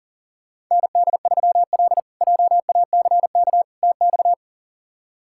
Morse code, heard as a keyed tone, about 40 wpm, 700 Hz. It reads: ND3L JACK TX
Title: K1USN SST QSO Practice - 40wpm